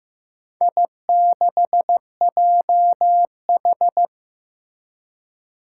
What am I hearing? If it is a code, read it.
Morse code, heard as a keyed tone, about 15 wpm, 700 Hz. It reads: I6JH
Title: Four Challenging Letter-Number Combinations 15wpm